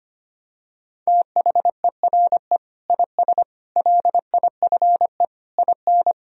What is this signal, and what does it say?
Morse code, heard as a keyed tone, about 25 wpm, 700 Hz. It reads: THERE IS LIFE IN
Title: Sentences from Top 1000 Words - Review Part 1 - 25wpm